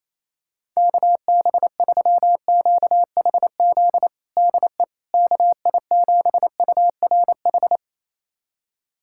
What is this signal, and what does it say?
Morse code, heard as a keyed tone, about 28 wpm, 700 Hz. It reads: KB3QHZ DE KI7URH
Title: USA Callsign DE Callsign - 6 Characters - 28wpm